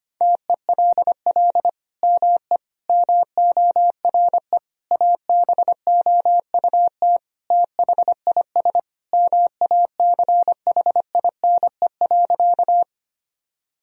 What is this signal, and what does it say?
Morse code, heard as a keyed tone, about 25 wpm, 700 Hz. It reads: TELL ME MORE ABOUT THIS MACHINE.
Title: Sentences from Top 1000 Words - Review Part 2 - 25wpm